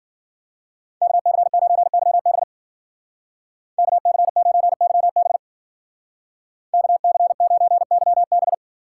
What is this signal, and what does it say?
Morse code, heard as a keyed tone, about 49 wpm, 700 Hz. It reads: KC9YB KC9YB KC9YB
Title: Speed Racing - USA 2x2 Call Signs 40wpm